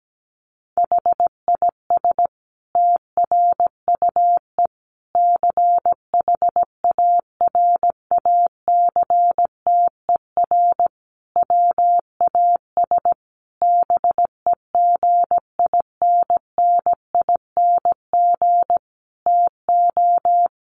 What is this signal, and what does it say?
Morse code, heard as a keyed tone, about 17 wpm, 700 Hz. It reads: HIS TRUE CHARACTER WAS BEGINNING TO
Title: Sentences from Top 1200 Words 17wpm